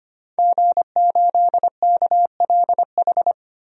Morse code, heard as a keyed tone, about 25 wpm, 700 Hz. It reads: G8KLH
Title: DXCC-100 Call Signs - 5 Characters - Repeated 3x - 25wpm